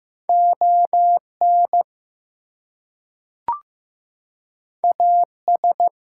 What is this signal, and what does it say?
Morse code, heard as a keyed tone, about 15 wpm, 700 Hz. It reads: ON AS
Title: Two Letter Words - 15wpm